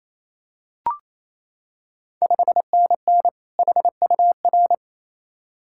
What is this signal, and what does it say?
Morse code, heard as a keyed tone, about 28 wpm, 700 Hz. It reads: 5NN HUR
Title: Ontario QSO Party Exchange 28wpm